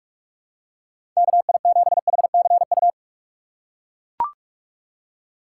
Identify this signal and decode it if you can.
Morse code, heard as a keyed tone, about 41 wpm, 700 Hz. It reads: KI7HCU
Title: USA Call Signs - 6 Characters - 2x Character Spacing - 45wpm